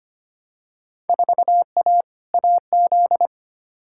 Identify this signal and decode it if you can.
Morse code, heard as a keyed tone, about 25 wpm, 700 Hz. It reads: 4A AZ
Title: Field Day Exchanges Encore - 25wpm